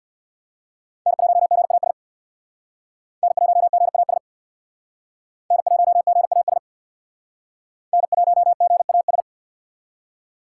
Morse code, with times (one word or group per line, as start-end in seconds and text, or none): 1.1-1.9 N1GAS
3.2-4.2 N1GAS
5.5-6.6 N1GAS
7.9-9.2 N1GAS